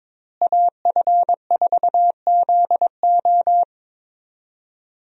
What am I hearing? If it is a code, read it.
Morse code, heard as a keyed tone, about 22 wpm, 700 Hz. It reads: AF4ZO
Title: USA Call Signs - 5 Characters - Repeated 3x - 22wpm